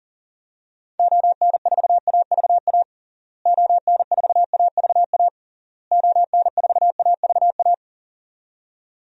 Morse code, heard as a keyed tone, about 40 wpm, 700 Hz. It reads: ON4AVA ON4AVA ON4AVA
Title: DXCC-100 Call Signs - 6 Characters - Repeated 3x - 40wpm